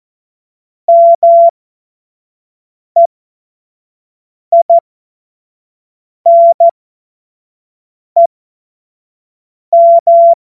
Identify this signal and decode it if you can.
Morse code, heard as a keyed tone, about 14 wpm, spaced as for 5 wpm, 680 Hz. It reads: MEINEM
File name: Witz-05-August-2025-25-F70